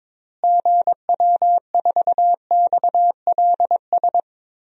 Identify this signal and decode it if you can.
Morse code, heard as a keyed tone, about 22 wpm, 700 Hz. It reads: GW4XLS
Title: DXCC-100 Call Signs - 6 Characters - Repeated 3x - 22wpm